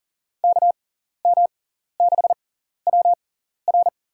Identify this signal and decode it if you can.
Morse code, heard as a keyed tone, about 40 wpm, spaced as for 14 wpm, 700 Hz. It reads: KM6WR
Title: USA Call Signs - 5 Characters - 6x Character Spacing - 40wpm